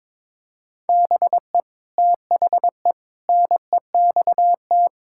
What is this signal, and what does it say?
Morse code, heard as a keyed tone, about 22 wpm, 700 Hz. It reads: BE THE NEXT
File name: practice / 22wpm / Sets of 8 Words - 22wpm